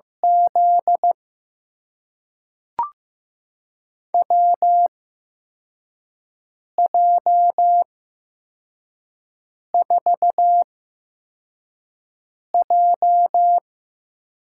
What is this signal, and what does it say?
Morse code, heard as a keyed tone, about 15 wpm, spaced as for 4 wpm, 700 Hz. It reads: Z WJ4J
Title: USA Call Signs - 8x Character Spacing - 15wpm